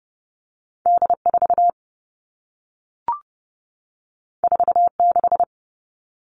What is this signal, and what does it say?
Morse code, heard as a keyed tone, about 30 wpm, 700 Hz. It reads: D4 46
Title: All Cut Numbers - 2-Digit 30wpm